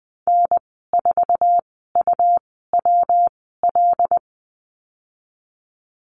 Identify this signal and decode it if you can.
Morse code, heard as a keyed tone, about 20 wpm, spaced as for 15 wpm, 700 Hz. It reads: N4UWL
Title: 167 Lesson 27g - Number 7 - Call Signs 20wpm